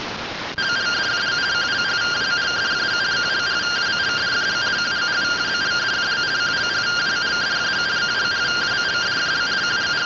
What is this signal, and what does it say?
Two-tone teleprinter signal, about 170 Hz shift.
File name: findings / AXIAL fr130 3mm 409.gen-37 rtty rx